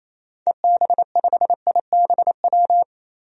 Morse code, heard as a keyed tone, about 28 wpm, 700 Hz. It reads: EB5IBW
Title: DXCC-100 Call Signs - 6 Characters - 28wpm